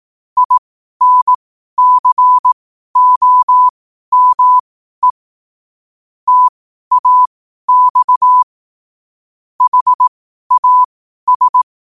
Morse code, heard as a keyed tone, about 18 wpm, spaced as for 12 wpm, 1000 Hz. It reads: INCOME TAX HAS
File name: qotdfiles / 12wpm